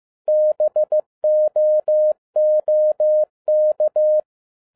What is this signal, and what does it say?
Morse code, heard as a keyed tone, about 15 wpm, 610 Hz. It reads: BOOK